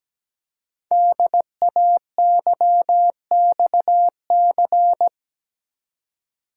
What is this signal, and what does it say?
Morse code, heard as a keyed tone, about 17 wpm, 700 Hz. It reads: DAYXC
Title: Code Groups - 5 Letters - 17wpm